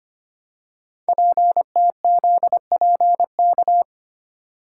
Morse code, heard as a keyed tone, about 25 wpm, 700 Hz. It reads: PTZPK
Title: Code Groups - 5 Letters - 25wpm